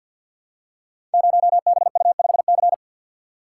Morse code, heard as a keyed tone, about 48 wpm, 700 Hz. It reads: 0BU5C
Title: Code Groups - 5 Letter:Numbers - 50wpm